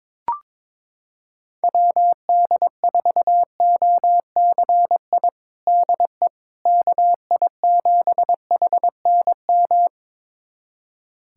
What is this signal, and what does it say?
Morse code, heard as a keyed tone, about 22 wpm, 700 Hz. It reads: WD4OCI DE KI7HNM
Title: USA Callsign DE Callsign - 6 Characters - 22wpm